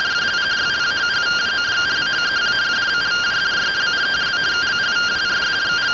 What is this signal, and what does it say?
Two-tone teleprinter signal, about 170 Hz shift.
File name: findings / 81 axial iodine 262.gen-10 rtty rx